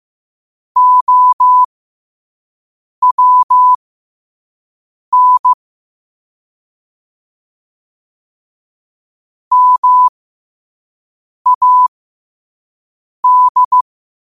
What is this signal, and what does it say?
Morse code, heard as a keyed tone, about 15 wpm, spaced as for 5 wpm, 1000 Hz. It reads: OWN MAD
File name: 5wpm